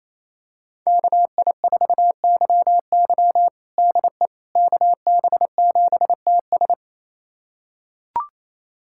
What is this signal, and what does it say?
Morse code, heard as a keyed tone, about 28 wpm, 700 Hz. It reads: KI4YY DE KB7TS
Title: USA Callsign DE Callsign - 5 Characters - 28wpm